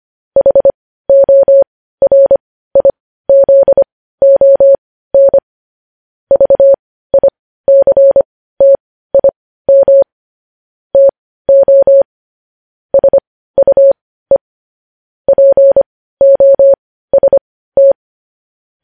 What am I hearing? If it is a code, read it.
Morse code, heard as a keyed tone, about 25 wpm, spaced as for 15 wpm, 550 Hz. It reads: HORIZON VICTIM TO SUE POST
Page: news headlines in morse code at 15 wpm